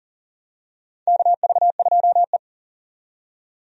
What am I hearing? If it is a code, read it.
Morse code, heard as a keyed tone, about 40 wpm, 700 Hz. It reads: KV2E
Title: USA Call Signs - 4 Characters - 40wpm